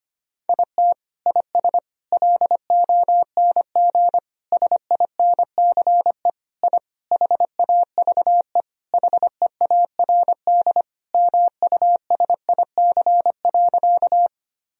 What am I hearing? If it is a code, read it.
Morse code, heard as a keyed tone, about 25 wpm, 700 Hz. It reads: IT IS LONG SINCE I HAVE HEARD MUSIC.
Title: Sentences from Top 1000 Words - Review Part 1 - 25wpm